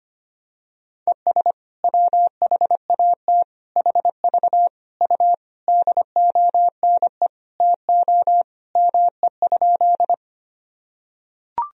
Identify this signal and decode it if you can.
Morse code, heard as a keyed tone, about 25 wpm, 700 Hz. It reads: ES WHAT HV U DONE TO ME?
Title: Sentences from Top 500 Words and Abbreviations 25wpm